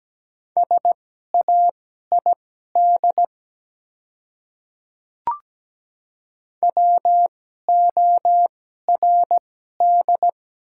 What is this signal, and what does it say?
Morse code, heard as a keyed tone, about 17 wpm, spaced as for 12 wpm, 700 Hz. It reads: SAID WORD
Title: 2x Character Spacing - Top 100 Words - 17wpm